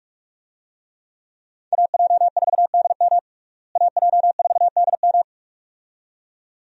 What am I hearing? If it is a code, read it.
Morse code, heard as a keyed tone, about 47 wpm, 700 Hz. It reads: AJ4DM AJ4DM
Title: USA Call Signs - 5 Characters - Repeated 2x - 45wpm